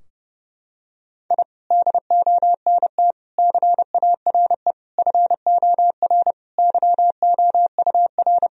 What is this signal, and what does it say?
Morse code, heard as a keyed tone, about 32 wpm, 700 Hz. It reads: I DONT CARE FOR YOUR
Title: Sentences from Top 1000 Words - Review Part 3 - 30wpm